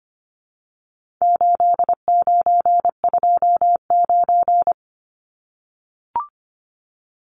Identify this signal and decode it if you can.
Morse code, heard as a keyed tone, about 25 wpm, 700 Hz. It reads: 8929
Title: Numbers - 4-Digit 25wpm